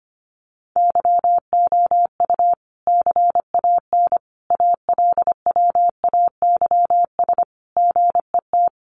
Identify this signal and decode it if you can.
Morse code, heard as a keyed tone, about 25 wpm, 700 Hz. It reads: YOU CAN ALWAYS GET
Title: Sentences from Top 1200 Words 25wpm